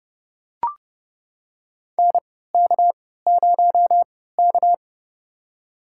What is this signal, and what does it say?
Morse code, heard as a keyed tone, about 30 wpm, spaced as for 17 wpm, 700 Hz. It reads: NK0K
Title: USA Call Signs - 4 Characters - 3x Character Spacing - 30wpm